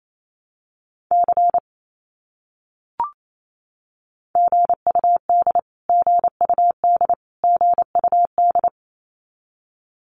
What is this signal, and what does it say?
Morse code, heard as a keyed tone, about 28 wpm, 700 Hz. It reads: C GUD GUD GUD
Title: Abbreviations Repeated 3x - 28wpm